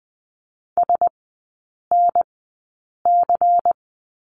Morse code, heard as a keyed tone, about 20 wpm, spaced as for 8 wpm, 700 Hz. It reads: SNC
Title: 178 Lesson 29e - Slash - Three Characters 20wpm